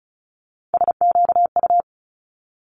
Morse code, heard as a keyed tone, about 36 wpm, 700 Hz. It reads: SQU
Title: Multi-letter Phonemes 35wpm